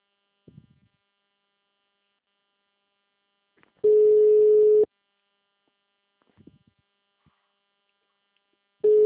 Ringback tone (Europe, 425 Hz).
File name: out- -2013- - - .10079